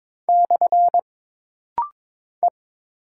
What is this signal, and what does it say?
Morse code, heard as a keyed tone, about 22 wpm, 700 Hz. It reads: / E
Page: This rapid-fire timing is geared to push your ICR (Instant Character Recognition) skills.